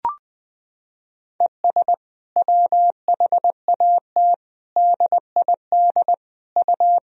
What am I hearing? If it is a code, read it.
Morse code, heard as a keyed tone, about 20 wpm, 700 Hz. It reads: ES WHAT DID U
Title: Sentences from Top 100 Words and Abbreviations 20wpm